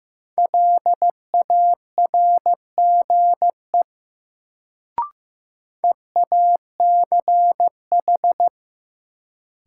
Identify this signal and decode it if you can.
Morse code, heard as a keyed tone, about 15 wpm, 700 Hz. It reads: LARGE EACH
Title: Sentences from Top 200 Words - New Word Drill - 15wpm